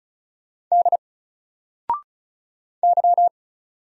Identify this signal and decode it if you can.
Morse code, heard as a keyed tone, about 36 wpm, 700 Hz. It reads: D Y
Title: Single Character (Letter:Number:Punctuation:Prosign) 35wpm